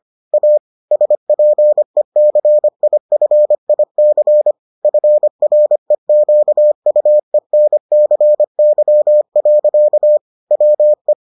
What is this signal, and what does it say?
Morse code, heard as a keyed tone, about 25 wpm, 600 Hz. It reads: A SPECIFIC FREQUENCY. WE